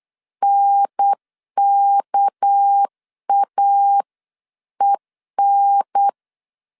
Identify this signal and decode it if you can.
Morse code, heard as a keyed tone, about 8 wpm, 800 Hz. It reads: NKAEN
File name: CW / DET / 1 6 1500C S B 8wpm 5142